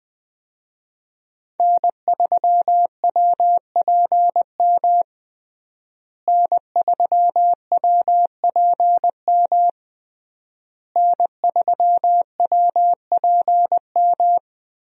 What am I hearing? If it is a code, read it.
Morse code, heard as a keyed tone, about 20 wpm, 700 Hz. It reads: N3WPM N3WPM N3WPM
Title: USA Call Signs - 5 Characters - Repeated 3x - 20wpm